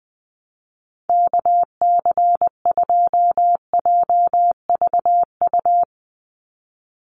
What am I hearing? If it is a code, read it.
Morse code, heard as a keyed tone, about 20 wpm, 700 Hz. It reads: KC2JVU
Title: USA Call Signs 20wpm